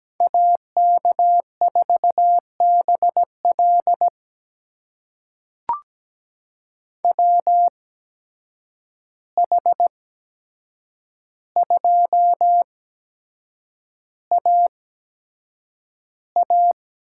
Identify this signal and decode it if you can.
Morse code, heard as a keyed tone, about 17 wpm, spaced as for 7 wpm, 700 Hz. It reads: AK4BL WH2AA
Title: USA Call Signs - 5 Characters - 8x Character Spacing - 17wpm